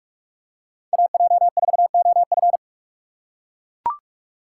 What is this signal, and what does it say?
Morse code, heard as a keyed tone, about 42 wpm, 700 Hz. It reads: AJ4OF
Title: USA Call Signs - 5 Characters - 4x Character Spacing - 45wpm